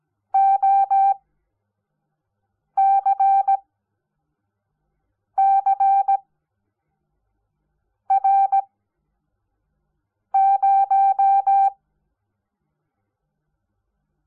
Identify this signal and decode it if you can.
Morse code, heard as a keyed tone, about 17 wpm, spaced as for 4 wpm, 780 Hz. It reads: OCCR0
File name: Onlinelehrgang / morsen / DJ4UF Morsen05